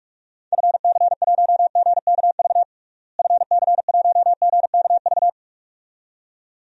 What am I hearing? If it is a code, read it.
Morse code, heard as a keyed tone, about 46 wpm, 700 Hz. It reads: FC1GKV FC1GKV
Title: DXCC-100 Call Signs - 6 Characters - Repeated 2x - 45wpm